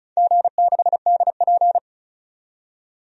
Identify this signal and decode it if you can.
Morse code, heard as a keyed tone, about 36 wpm, 700 Hz. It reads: G6DP
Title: DXCC-100 Call Signs - 4 Characters - 35wpm